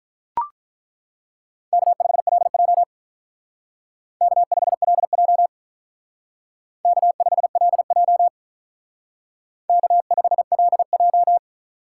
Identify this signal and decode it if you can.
Morse code, heard as a keyed tone, about 42 wpm, 700 Hz. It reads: K5LJ K5LJ K5LJ K5LJ
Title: Speed Racing - USA 1x2 Call Signs 35wpm